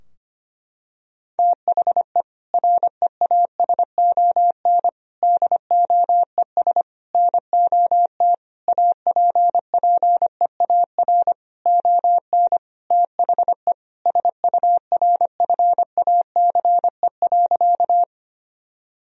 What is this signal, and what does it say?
Morse code, heard as a keyed tone, about 25 wpm, 700 Hz. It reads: THE REASON DOES NOT APPEAR ON THE SURFACE.
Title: Sentences from Top 1000 Words - Review Part 3 - 25wpm